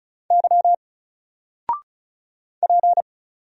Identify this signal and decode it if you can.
Morse code, heard as a keyed tone, about 36 wpm, 700 Hz. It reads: Y P
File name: Single-Letter-Number-35wpm